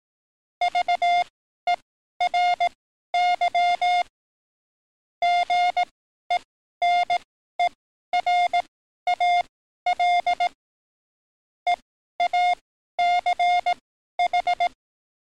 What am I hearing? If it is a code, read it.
Morse code, heard as a keyed tone, about 18 wpm, spaced as for 12 wpm, 710 Hz. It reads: VERY GENERAL EACH
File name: training / NAVPERS 11049to11060 / NAVPERS 11053 RA RB